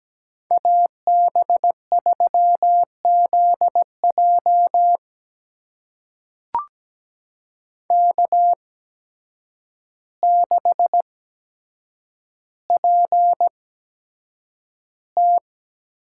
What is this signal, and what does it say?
Morse code, heard as a keyed tone, about 17 wpm, spaced as for 8 wpm, 700 Hz. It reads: AB3ZJ K6PT
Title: USA Call Signs - 8x Character Spacing - 17wpm